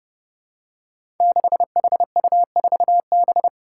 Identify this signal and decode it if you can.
Morse code, heard as a keyed tone, about 32 wpm, 700 Hz. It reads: 6HU4B
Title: Five Challenging Letter-Number Combinations 30wpm